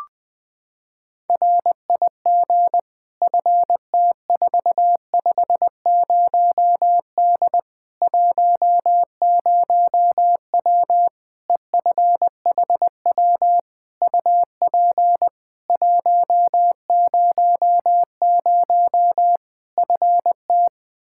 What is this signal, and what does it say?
Morse code, heard as a keyed tone, about 20 wpm, 700 Hz. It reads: RIG FT450D 10W EFHW UP 100 FT
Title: K1USN SST QSO Practice 20wpm